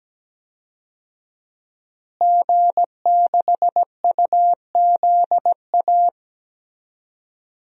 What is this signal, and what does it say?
Morse code, heard as a keyed tone, about 17 wpm, 700 Hz. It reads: G6UZA
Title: DXCC-100 Call Signs - 5 Characters - Repeated 2x - 17wpm